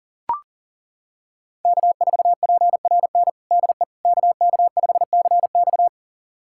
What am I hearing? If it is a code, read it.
Morse code, heard as a keyed tone, about 40 wpm, 700 Hz. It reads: K4PRN DE KK5CX
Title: USA Callsign DE Callsign - 5 Characters - 40wpm